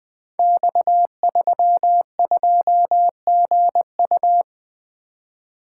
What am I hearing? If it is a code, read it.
Morse code, heard as a keyed tone, about 20 wpm, 700 Hz. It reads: X32GU
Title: Code Groups - 5 Letter:Numbers - 20wpm